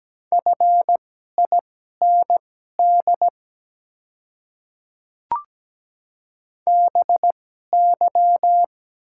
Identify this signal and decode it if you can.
Morse code, heard as a keyed tone, about 17 wpm, spaced as for 12 wpm, 700 Hz. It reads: FIND BY
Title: 2x Character Spacing - Top 100 Words - 17wpm